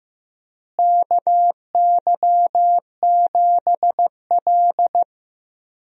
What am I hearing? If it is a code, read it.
Morse code, heard as a keyed tone, about 16 wpm, 700 Hz. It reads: KY7L
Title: USA Call Signs - 4 Characters - Repeated 2x - 15wpm